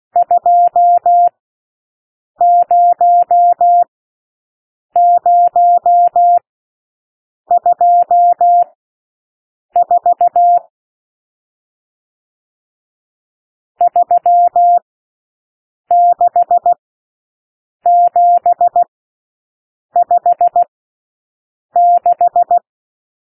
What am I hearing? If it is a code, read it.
Morse code, heard as a keyed tone, about 16 wpm, spaced as for 6 wpm, 700 Hz. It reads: 20024 36756
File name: SOUND NUMBERS 16 wpm